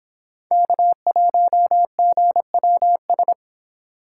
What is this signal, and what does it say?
Morse code, heard as a keyed tone, about 29 wpm, 700 Hz. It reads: K1GWS